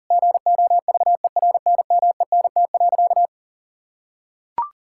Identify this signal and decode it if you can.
Morse code, heard as a keyed tone, about 37 wpm, 700 Hz. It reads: GOVERNMENT.
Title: Sentences from Top 1000 Words - Review Part 6 - 40wpm